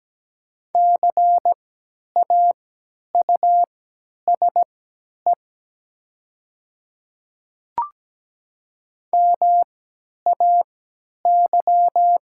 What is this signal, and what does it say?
Morse code, heard as a keyed tone, about 17 wpm, spaced as for 10 wpm, 700 Hz. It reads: CAUSE MAY
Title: 3x Character Spacing - Top 200 Words - 17wpm